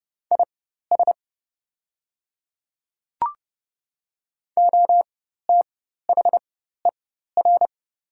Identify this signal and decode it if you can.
Morse code, heard as a keyed tone, about 29 wpm, spaced as for 14 wpm, 700 Hz. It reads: IS OTHER
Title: 4x Character Spacing - Top 100 Words - 30wpm